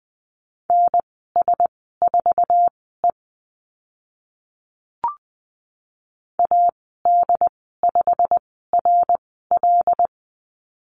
Morse code, heard as a keyed tone, about 20 wpm, spaced as for 15 wpm, 700 Hz. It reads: NS4E AD5RL
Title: 088 Lesson 15g - Number 5 - Call Signs 20wpm